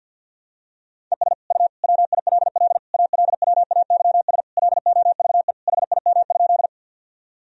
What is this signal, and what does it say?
Morse code, heard as a keyed tone, about 50 wpm, 700 Hz. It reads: ES U WILL ALWAYS LOVE HIM?
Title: Sentences from Top 400 Words and Abbreviations 50wpm